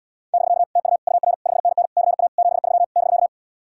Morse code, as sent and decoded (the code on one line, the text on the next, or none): -...-.- / - ..- / ..- .-. / ..... -. -. / -.-. .- / --... ...-- / -...-.-
<BK> TU UR 5NN CA 73 <BK>